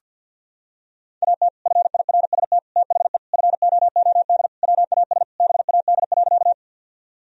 Morse code, heard as a keyed tone, about 46 wpm, 700 Hz. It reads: AT FIRST THE FOOD WAS BAD.
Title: Sentences from Top 1000 Words - Review Part 1 - 50wpm